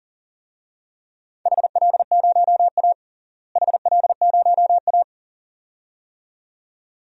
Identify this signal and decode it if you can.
Morse code, heard as a keyed tone, about 40 wpm, 700 Hz. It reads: HL0A HL0A
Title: DXCC-100 Call Signs - 4 Characters - Repeated 2x - 40wpm